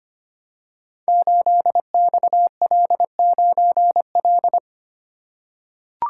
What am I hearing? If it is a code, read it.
Morse code, heard as a keyed tone, about 25 wpm, 700 Hz. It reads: 8XL9L
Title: Code Groups - 5 Letter:Numbers - 25wpm